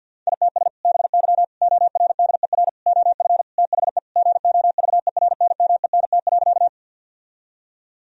Morse code, heard as a keyed tone, about 50 wpm, 700 Hz. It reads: ITS BY ORDER OF THE GOVERNMENT.
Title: Sentences from Top 1000 Words - Review Part 3 - 50wpm